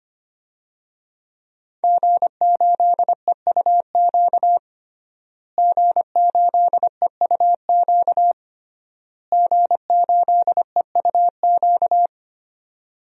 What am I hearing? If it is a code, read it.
Morse code, heard as a keyed tone, about 25 wpm, 700 Hz. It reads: G8EUQ G8EUQ G8EUQ
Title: DXCC-100 Call Signs - 5 Characters - Repeated 3x - 25wpm